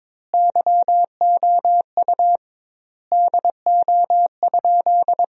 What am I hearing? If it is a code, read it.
Morse code, heard as a keyed tone, about 22 wpm, 700 Hz. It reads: YOU DO?
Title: 2x Word-Spacing - Sentences from Top 100 Words - 22wpm